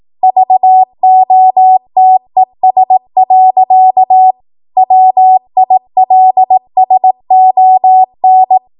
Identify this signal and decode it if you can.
Morse code, heard as a keyed tone, about 18 wpm, 750 Hz. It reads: VOTES. WILSON